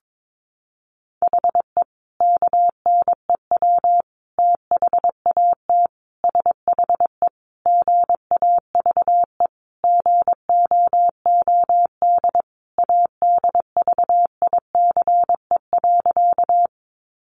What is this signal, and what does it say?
Morse code, heard as a keyed tone, about 22 wpm, 700 Hz. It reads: HE KNEW THAT SHE GAVE GOOD ADVICE.
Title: Sentences from Top 1100 Words 22wpm